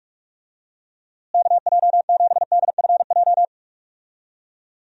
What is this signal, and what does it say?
Morse code, heard as a keyed tone, about 46 wpm, 700 Hz. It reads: KJ7DFJ
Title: USA Call Signs - 6 Characters - Repeated 2x - 45wpm